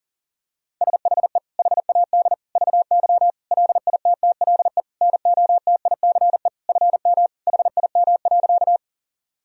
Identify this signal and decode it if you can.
Morse code, heard as a keyed tone, about 40 wpm, 700 Hz. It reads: SHE HAD VY LITTLE NOTICE FM HIM.
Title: Sentences from Top 400 Words and Abbreviations 40wpm